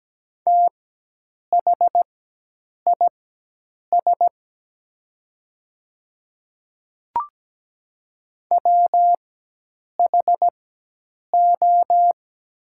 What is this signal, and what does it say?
Morse code, heard as a keyed tone, about 17 wpm, spaced as for 8 wpm, 690 Hz. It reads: THIS WHO
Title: 4x Character Spacing - Top 300 Words - 17wpm